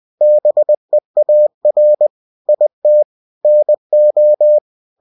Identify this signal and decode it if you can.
Morse code, heard as a keyed tone, about 20 wpm, 600 Hz. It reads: BEAR IT NO
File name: cw-books / winnie-the-pooh / winnie-20wpm-page-0040